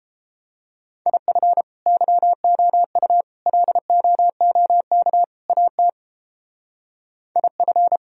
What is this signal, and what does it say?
Morse code, heard as a keyed tone, about 29 wpm, 700 Hz. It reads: IF YOU LOOK AT IF